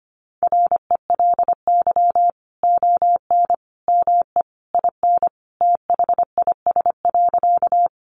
Morse code, heard as a keyed tone, about 25 wpm, 700 Hz. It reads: RELY ON ME IN THIS.
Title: Sentences from Top 1200 Words 25wpm